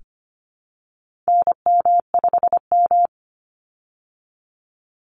Morse code, heard as a keyed tone, about 25 wpm, 700 Hz. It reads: NM5M
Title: Call Signs Repeated 2x - 25wpm